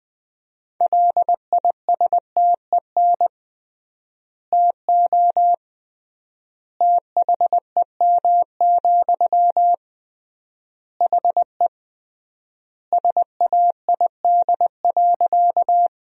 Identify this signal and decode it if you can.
Morse code, heard as a keyed tone, about 20 wpm, 700 Hz. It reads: LISTEN TO THEM, HE SAID.
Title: 3x Word-Spacing - Sentences from Top 400 Words - 20wpm